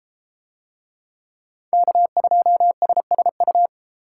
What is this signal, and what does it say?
Morse code, heard as a keyed tone, about 33 wpm, 700 Hz. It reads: K2SSU